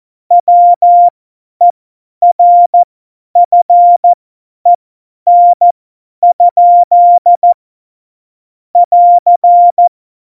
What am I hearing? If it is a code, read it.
Morse code, heard as a keyed tone, about 14 wpm, spaced as for 10 wpm, 700 Hz. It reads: WERFEN? <AR>
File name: Morsekurs / Witze / Witz 25. Februar 2020, 50 F70